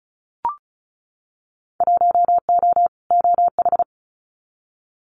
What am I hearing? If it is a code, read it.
Morse code, heard as a keyed tone, about 34 wpm, 700 Hz. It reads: 1O OH
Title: Winter Field Day Exchanges 35wpm